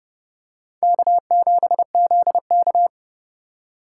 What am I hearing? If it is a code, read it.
Morse code, heard as a keyed tone, about 31 wpm, 700 Hz. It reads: K7ZK